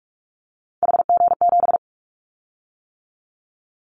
Morse code, heard as a keyed tone, about 44 wpm, 700 Hz. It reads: HG7
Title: Callsign Prefixes for Top 25 DXCC - 45wpm